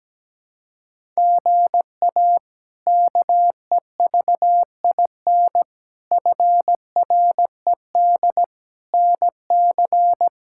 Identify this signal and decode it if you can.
Morse code, heard as a keyed tone, about 17 wpm, 700 Hz. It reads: GA KEVIN FRED NC
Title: K1USN SST QSO Practice - 17wpm